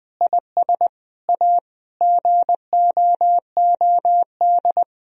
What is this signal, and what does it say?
Morse code, heard as a keyed tone, about 21 wpm, 700 Hz. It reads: IS A GOOD
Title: Sentences from Top 1000 Words - Review Part 3 - 20wpm